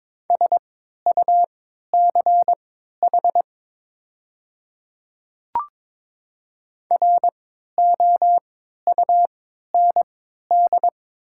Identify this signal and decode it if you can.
Morse code, heard as a keyed tone, about 22 wpm, spaced as for 13 wpm, 700 Hz. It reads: SUCH ROUND
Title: 3x Character Spacing - Top 300 Words - 22wpm